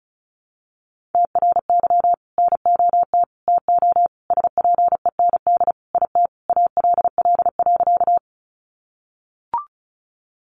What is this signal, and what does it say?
Morse code, heard as a keyed tone, about 34 wpm, 700 Hz. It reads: TRY NOT TO SPEND IT ALL.
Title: Sentences from Top 800 Words 35wpm